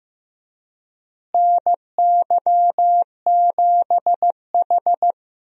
Morse code, heard as a keyed tone, about 16 wpm, 700 Hz. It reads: NY7H
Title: USA Call Signs - 4 Characters - Repeated 2x - 15wpm